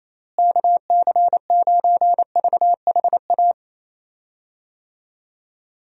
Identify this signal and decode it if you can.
Morse code, heard as a keyed tone, about 27 wpm, 700 Hz. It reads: KC9VHA
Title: USA Call Signs - 6 Characters - Repeated 3x - 28wpm